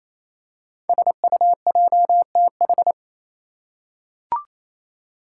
Morse code, heard as a keyed tone, about 27 wpm, 700 Hz. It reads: SUJTH
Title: Code Groups - 5 Letter:Numbers - 28wpm